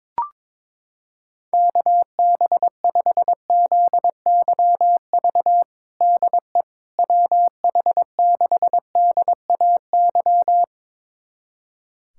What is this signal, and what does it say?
Morse code, heard as a keyed tone, about 21 wpm, 700 Hz. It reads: KB5ZYV DE WH6DAY
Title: USA Callsign DE Callsign - 6 Characters - 22wpm